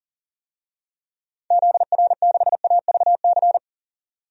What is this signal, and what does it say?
Morse code, heard as a keyed tone, about 40 wpm, 700 Hz. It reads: ZR6AVC
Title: DXCC-100 Call Signs - 6 Characters - 40wpm